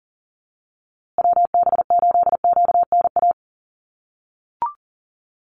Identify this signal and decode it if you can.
Morse code, heard as a keyed tone, about 40 wpm, 700 Hz. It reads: WB8QNA
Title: USA Call Signs Repeated 3x 40wpm